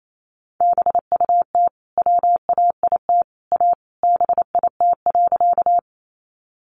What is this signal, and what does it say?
Morse code, heard as a keyed tone, about 28 wpm, 700 Hz. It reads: BUT WAIT A BIT.
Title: Sentences from Top 900 Words 28wpm